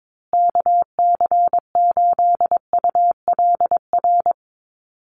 Morse code, heard as a keyed tone, about 22 wpm, 700 Hz. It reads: KC8ULR
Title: USA Call Signs Repeated 2x 22wpm